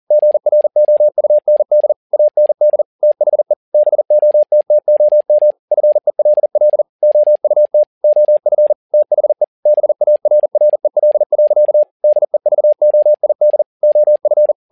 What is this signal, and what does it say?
Morse code, heard as a keyed tone, about 40 wpm, 600 Hz. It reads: GROUND AND THE BOTTOM FELL OUT OF THE BARREL. DEVOID OF
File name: learning center / tower trouble / tower touble 40wpm